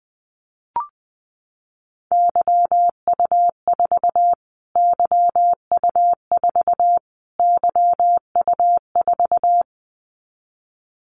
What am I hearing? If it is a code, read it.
Morse code, heard as a keyed tone, about 19 wpm, 700 Hz. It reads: YU4 YU4 YU4
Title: Callsign Prefixes for Top 25 DXCC - Repeated 3x - 20wpm